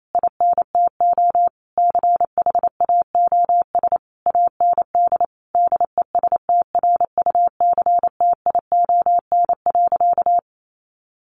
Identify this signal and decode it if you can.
Morse code, heard as a keyed tone, about 28 wpm, 700 Hz. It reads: INTO CHAOS AND DESTRUCTION.
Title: A Sound Of Thunder 28wpm